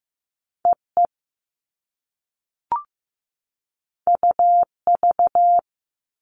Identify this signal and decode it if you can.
Morse code, heard as a keyed tone, about 15 wpm, 700 Hz. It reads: EE UV
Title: All Cut Numbers - 2-Digit 15wpm